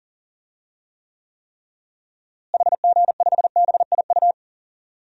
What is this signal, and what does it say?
Morse code, heard as a keyed tone, about 40 wpm, 700 Hz. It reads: HG5BIU
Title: DXCC-100 Call Signs - 6 Characters - Repeated 3x - 40wpm